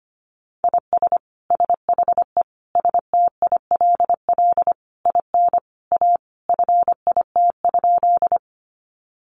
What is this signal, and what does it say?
Morse code, heard as a keyed tone, about 25 wpm, 700 Hz. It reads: IS SHE STILL IN A FIT?
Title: Sentences from Top 800 Words 25wpm